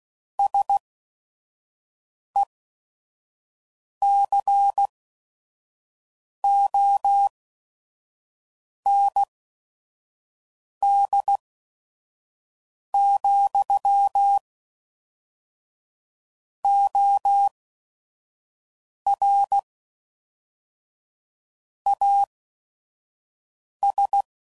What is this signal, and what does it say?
Morse code, heard as a keyed tone, about 16 wpm, spaced as for 5 wpm, 800 Hz. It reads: SECOND, OR AS
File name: CW Code Groups @5WPM / Technical Subjects / Reactance